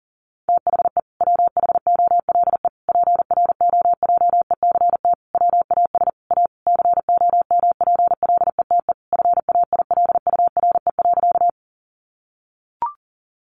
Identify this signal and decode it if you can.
Morse code, heard as a keyed tone, about 40 wpm, 700 Hz. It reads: THE WHOLE PROJECT WAS A COMPLETE FAILURE.
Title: Sentences from Top 1400 Words 40wpm